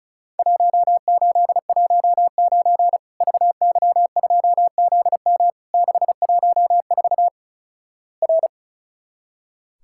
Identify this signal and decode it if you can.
Morse code, heard as a keyed tone, about 36 wpm, 700 Hz. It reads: 1819 VY2ZM 614 R
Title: Worked All Europe DX Contest - QTC Exchange - 35wpm